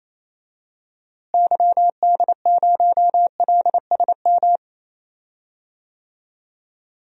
Morse code, heard as a keyed tone, about 28 wpm, 700 Hz. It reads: YD0LSM
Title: DXCC-100 Call Signs - 6 Characters - Repeated 3x - 28wpm